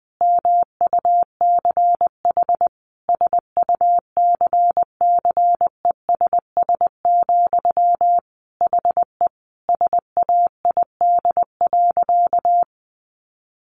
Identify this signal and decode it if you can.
Morse code, heard as a keyed tone, about 20 wpm, 700 Hz. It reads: MUCH SUCCESS, HE SAID.
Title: Sentences from Top 900 Words 20wpm